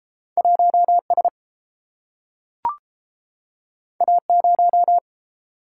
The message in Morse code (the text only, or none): 1S A0